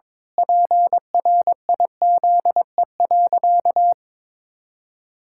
Morse code, heard as a keyed tone, about 21 wpm, 700 Hz. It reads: PRIZE.
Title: Pangram Challenge v2 - 22wpm